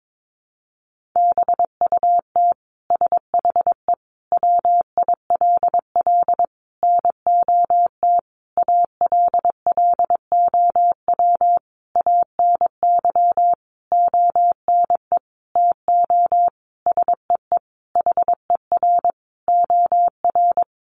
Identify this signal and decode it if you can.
Morse code, heard as a keyed tone, about 22 wpm, 700 Hz. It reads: BUT SHE WILL NOT ALLOW ANY ONE TO SEE HER OR
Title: Sentences from Top 1200 Words 22wpm